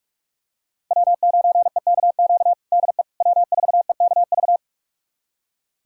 Morse code, heard as a keyed tone, about 46 wpm, 700 Hz. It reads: W9EKQ DE W4EKV
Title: USA Callsign DE Callsign - 5 Characters - 45wpm